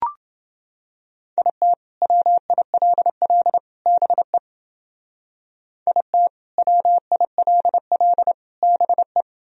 Morse code, heard as a keyed tone, about 29 wpm, 700 Hz. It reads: IT WILL BE IT WILL BE